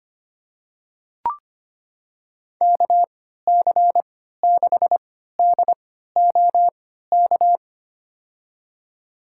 Morse code, heard as a keyed tone, about 25 wpm, spaced as for 14 wpm, 700 Hz. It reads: KC6DOK
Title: USA Call Signs - 6 Characters - 3x Character Spacing - 25wpm